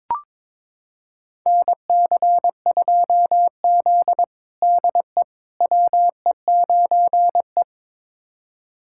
Morse code, heard as a keyed tone, about 22 wpm, 700 Hz. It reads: NC2Z DE WE9E
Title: USA Callsign DE Callsign - 4 Characters - 22wpm